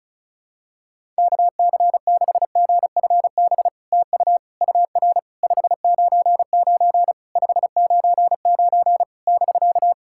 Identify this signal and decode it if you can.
Morse code, heard as a keyed tone, about 36 wpm, 700 Hz. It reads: KC6GFB TU UR 599 599 <BK>
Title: SOTA QSO Practice 35wpm